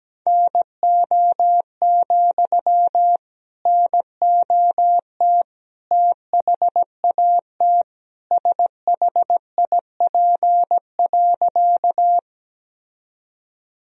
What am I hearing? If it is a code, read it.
Morse code, heard as a keyed tone, about 17 wpm, 700 Hz. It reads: NO, NOT THAT SHIP.
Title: Sentences from Top 1000 Words - Review Part 2 - 17wpm